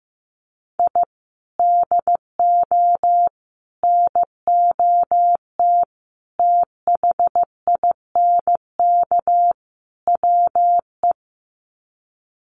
Morse code, heard as a keyed tone, about 15 wpm, 700 Hz. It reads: I DO NOT THINK WE
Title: Sets of 5 Words 15wpm